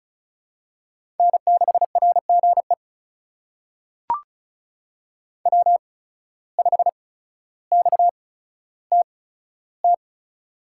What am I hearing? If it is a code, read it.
Morse code, heard as a keyed tone, about 36 wpm, spaced as for 15 wpm, 700 Hz. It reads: N6RGE W5XTT
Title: USA Call Signs - 5 Characters - 8x Character Spacing - 35wpm